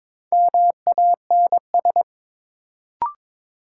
Morse code, heard as a keyed tone, about 22 wpm, 700 Hz. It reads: MANS
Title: Sentences from Top 1000 Words - Review Part 6 - New Word Drill - 22wpm